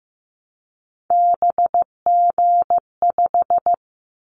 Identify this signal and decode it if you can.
Morse code, heard as a keyed tone, about 15 wpm, 700 Hz. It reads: BG5
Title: Callsign Prefixes for Top 100 DXCC - 15wpm